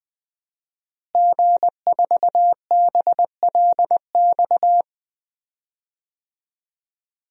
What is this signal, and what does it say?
Morse code, heard as a keyed tone, about 21 wpm, 700 Hz. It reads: G4BLX
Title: DXCC-100 Call Signs - 5 Characters - Repeated 2x - 20wpm